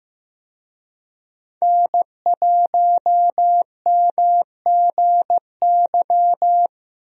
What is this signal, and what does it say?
Morse code, heard as a keyed tone, about 16 wpm, 700 Hz. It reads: N1MGY
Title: USA Call Signs - 5 Characters - 15wpm